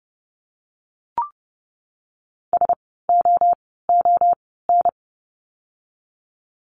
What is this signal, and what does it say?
Morse code, heard as a keyed tone, about 30 wpm, spaced as for 17 wpm, 700 Hz. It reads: SOON
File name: Lesson 16f - Letter c - Words 30wpm